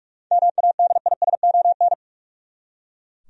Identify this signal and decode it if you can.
Morse code, heard as a keyed tone, about 41 wpm, 700 Hz. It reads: MADISON
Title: USA Common Names - 45wpm